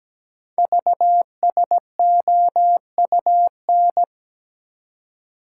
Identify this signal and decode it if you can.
Morse code, heard as a keyed tone, about 17 wpm, 700 Hz. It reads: VSOUN
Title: Code Groups - 5 Letters - 17wpm